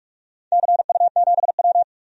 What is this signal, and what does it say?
Morse code, heard as a keyed tone, about 44 wpm, 690 Hz. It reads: CU7W
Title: DXCC-100 Call Signs - 4 Characters - 45wpm